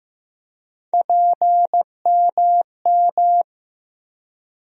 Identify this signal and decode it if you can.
Morse code, heard as a keyed tone, about 15 wpm, 700 Hz. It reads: PMM
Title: USA Call Sign Postfixes - 3 Letters 15wpm